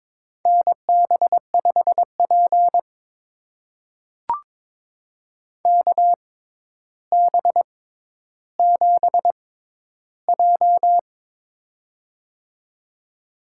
Morse code, heard as a keyed tone, about 22 wpm, spaced as for 11 wpm, 700 Hz. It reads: NB5P KB7J
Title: USA Call Signs - 4 Characters - 6x Character Spacing - 22wpm